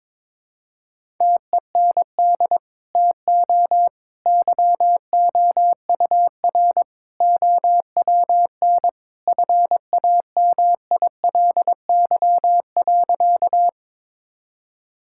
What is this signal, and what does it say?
Morse code, heard as a keyed tone, about 21 wpm, 700 Hz. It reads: TEND TO YOUR OWN FAMILY.
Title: Sentences from Top 1000 Words - Review Part 5 - 22wpm